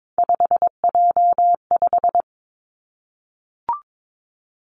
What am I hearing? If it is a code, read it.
Morse code, heard as a keyed tone, about 22 wpm, 700 Hz. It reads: 5J5
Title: Three Challenging Letter-Number Combinations 22wpm